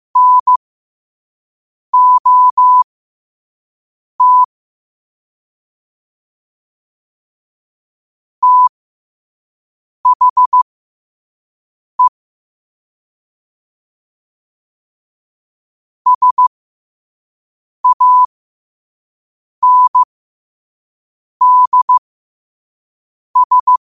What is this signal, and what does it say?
Morse code, heard as a keyed tone, about 15 wpm, spaced as for 5 wpm, 1000 Hz. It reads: NOT THE SANDS
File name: qotdfiles / 5wpm